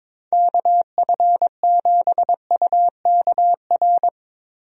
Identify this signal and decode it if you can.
Morse code, heard as a keyed tone, about 22 wpm, 700 Hz. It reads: KF7UKR
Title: USA Call Signs - 6 Characters - Repeated 2x - 22wpm